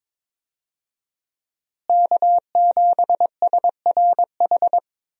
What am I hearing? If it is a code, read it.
Morse code, heard as a keyed tone, about 22 wpm, 700 Hz. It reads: K7SRH
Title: USA Call Signs - 5 Characters - Repeated 3x - 22wpm